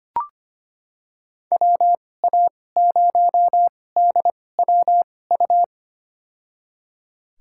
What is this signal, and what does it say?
Morse code, heard as a keyed tone, about 25 wpm, spaced as for 18 wpm, 700 Hz. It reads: WA0DWU
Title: USA Call Signs - 6 Characters - 2x Character Spacing - 25wpm